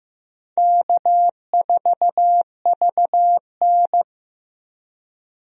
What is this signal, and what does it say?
Morse code, heard as a keyed tone, about 15 wpm, 700 Hz. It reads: K4VN
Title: USA Call Signs - 4 Characters - Repeated 2x - 15wpm